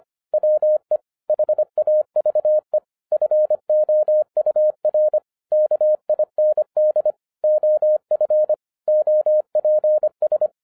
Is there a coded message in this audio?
Yes: Morse code from 0.3 to 10.5 s, heard as a keyed tone, about 25 wpm, 610 Hz. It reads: WE HAVE FOUR KIND OF OPS